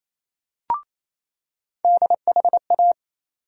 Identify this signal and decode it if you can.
Morse code, heard as a keyed tone, about 28 wpm, 700 Hz. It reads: DHA
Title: USA Call Sign Postfixes - 3 Letters 28wpm